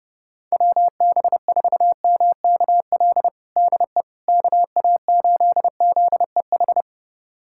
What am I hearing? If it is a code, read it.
Morse code, heard as a keyed tone, about 32 wpm, 700 Hz. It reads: WB4MKL DE KA8ZEH
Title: USA Callsign DE Callsign - 6 Characters - 30wpm